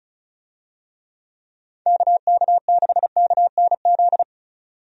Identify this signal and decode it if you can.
Morse code, heard as a keyed tone, about 34 wpm, 700 Hz. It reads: KK6KNZ
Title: USA Call Signs - 6 Characters - 35wpm